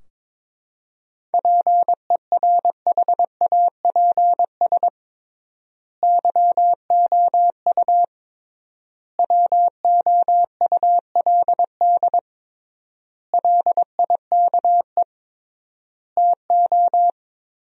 Morse code, heard as a keyed tone, about 22 wpm, 700 Hz. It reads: PERHAPS YOU WOULD LIKE TO
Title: 3x Word-Spacing - Sentences from Top 500 Words - 22wpm